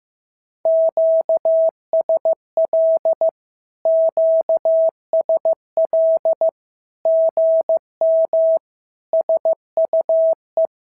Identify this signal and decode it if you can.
Morse code, heard as a keyed tone, about 15 wpm, 650 Hz. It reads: QSL QSL GM SUE
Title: SOTA QSO Practice 15wpm